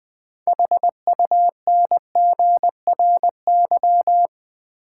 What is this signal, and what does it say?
Morse code, heard as a keyed tone, about 20 wpm, 700 Hz. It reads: HUNGRY
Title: Sentences from Top 1400 Words - New Word Drill - 20wpm